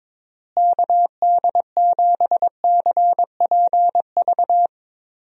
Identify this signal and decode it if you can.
Morse code, heard as a keyed tone, about 21 wpm, 700 Hz. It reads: KD7CPV
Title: USA Call Signs - 6 Characters - Repeated 2x - 22wpm